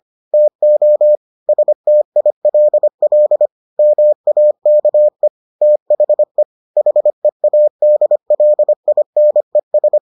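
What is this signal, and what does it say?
Morse code, heard as a keyed tone, about 25 wpm, 600 Hz. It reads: TO STILL MAKE THE HEADLINES
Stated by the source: Morse edition of the Foundations of Amateur Radio episodes.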